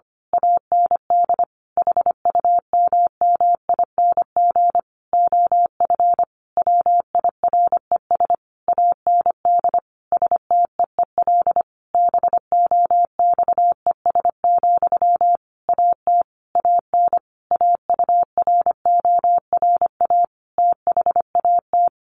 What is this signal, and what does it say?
Morse code, heard as a keyed tone, about 25 wpm, 700 Hz. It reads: AND HUMMING OF WIRES AND STEEL BOXES, AT AN AURORA THAT
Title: A Sound Of Thunder 25wpm